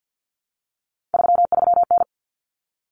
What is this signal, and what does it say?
Morse code, heard as a keyed tone, about 50 wpm, 700 Hz. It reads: 33N
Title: Common Cut Numbers - 3-Digit 50wpm